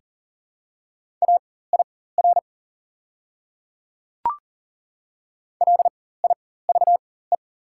Morse code, heard as a keyed tone, about 37 wpm, spaced as for 18 wpm, 700 Hz. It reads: AIR LIVE
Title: 4x Character Spacing - Top 200 Words - 40wpm